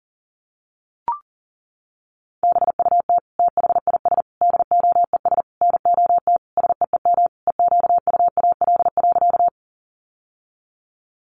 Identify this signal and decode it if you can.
Morse code, heard as a keyed tone, about 40 wpm, 700 Hz. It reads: BUT THIS DOES NOT SEEM EQUAL.
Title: Sentences from Top 1400 Words 40wpm